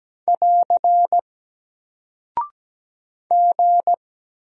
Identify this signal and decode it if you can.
Morse code, heard as a keyed tone, about 17 wpm, 700 Hz. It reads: <AR> G
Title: Single Character (Letter:Number:Punctuation:Prosign) Alt 17wpm